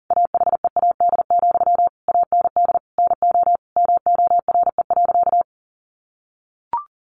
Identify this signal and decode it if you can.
Morse code, heard as a keyed tone, about 40 wpm, 700 Hz. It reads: AHEAD, AND NO MORE.
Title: Sentences from Top 1100 Words 40wpm